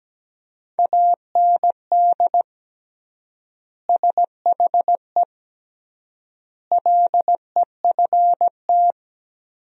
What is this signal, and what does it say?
Morse code, heard as a keyed tone, about 17 wpm, 700 Hz. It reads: AND SHE LEFT
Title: 3x Word-Spacing - Sentences from Top 300 Words - 17wpm